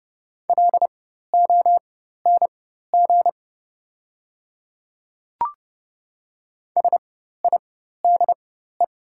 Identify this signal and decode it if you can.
Morse code, heard as a keyed tone, about 30 wpm, spaced as for 14 wpm, 700 Hz. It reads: LONG SIDE
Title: 4x Character Spacing - Top 100 Words - 30wpm